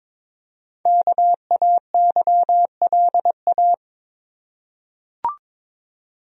Morse code, heard as a keyed tone, about 22 wpm, 700 Hz. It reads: KAYLA
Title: USA Common Names - Max 5 Letters - 22wpm